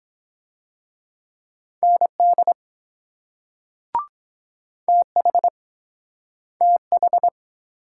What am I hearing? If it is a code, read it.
Morse code, heard as a keyed tone, about 23 wpm, 700 Hz. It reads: ND TH TH